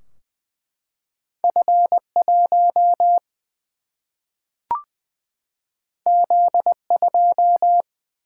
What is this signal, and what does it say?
Morse code, heard as a keyed tone, about 20 wpm, 700 Hz. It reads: F 1 Z 2
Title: Single-Letter Single-Number 20wpm